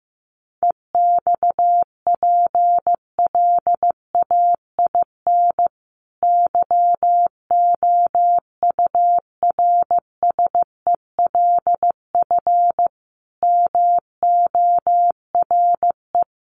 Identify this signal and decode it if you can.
Morse code, heard as a keyed tone, about 15 wpm, 700 Hz. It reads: EXPLAIN YOURSELF MORE
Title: Sentences from Top 1300 Words 15wpm